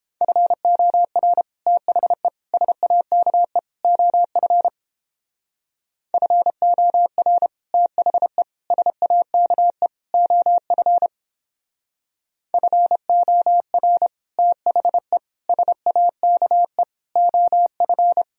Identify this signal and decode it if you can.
Morse code, heard as a keyed tone, about 29 wpm, 700 Hz. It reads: FOR THE SAKE OF FOR THE SAKE OF FOR THE SAKE OF